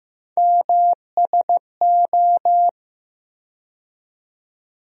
Morse code, heard as a keyed tone, about 15 wpm, 700 Hz. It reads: MSO
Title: USA Call Sign Postfixes - 3 Letters 15wpm